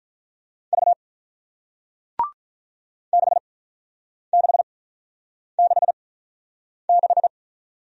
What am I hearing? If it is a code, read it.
Morse code, heard as a keyed tone, about 48 wpm, spaced as for 9 wpm, 700 Hz. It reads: V 6666